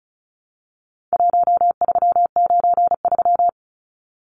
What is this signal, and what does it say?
Morse code, heard as a keyed tone, about 34 wpm, 700 Hz. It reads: 1393
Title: Numbers - 4-Digit 35wpm